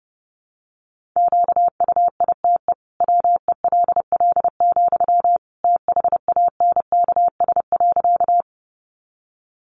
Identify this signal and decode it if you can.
Morse code, heard as a keyed tone, about 30 wpm, 700 Hz. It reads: QUITE WELL, THANKS.
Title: Sentences from Top 700 Words 30wpm